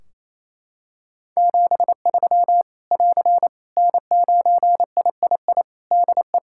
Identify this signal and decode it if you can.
Morse code, heard as a keyed tone, about 27 wpm, 700 Hz. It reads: 73 <AR> N9III DE
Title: Practice QSO - Exchange 4 - Ending - 28wpm